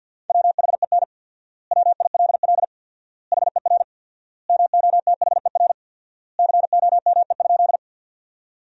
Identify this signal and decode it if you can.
Morse code, heard as a keyed tone, about 50 wpm, 700 Hz. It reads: WHEN WILL HER MOTHER COME?
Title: 4x Word-Spacing - Sentences from Top 200 Words - 50wpm